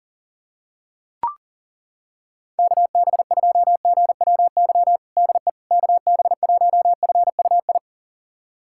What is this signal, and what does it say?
Morse code, heard as a keyed tone, about 37 wpm, 700 Hz. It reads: KB2GWY DE KB1FUI
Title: USA Callsign DE Callsign - 6 Characters - 40wpm